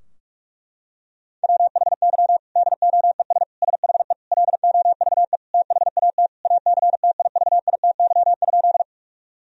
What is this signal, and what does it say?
Morse code, heard as a keyed tone, about 46 wpm, 700 Hz. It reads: WHY DOES SHE LOVE THAT ACTIVITY?
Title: Sentences from Top 1000 Words - Review Part 4 - 45wpm